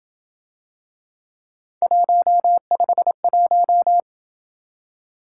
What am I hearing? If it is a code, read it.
Morse code, heard as a keyed tone, about 26 wpm, 700 Hz. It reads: 151